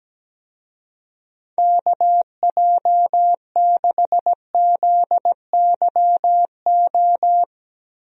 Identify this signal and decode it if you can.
Morse code, heard as a keyed tone, about 17 wpm, 700 Hz. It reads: KJ6ZYO
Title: USA Call Signs - 6 Characters - Repeated 3x - 17wpm